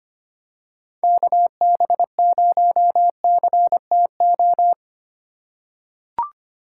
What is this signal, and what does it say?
Morse code, heard as a keyed tone, about 25 wpm, 700 Hz. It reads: KB0CTO
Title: USA Call Signs - 6 Characters - 3x Character Spacing - 25wpm